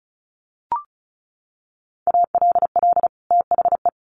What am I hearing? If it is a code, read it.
Morse code, heard as a keyed tone, about 34 wpm, 700 Hz. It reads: ALL THE
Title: Sets of 2 Words 35wpm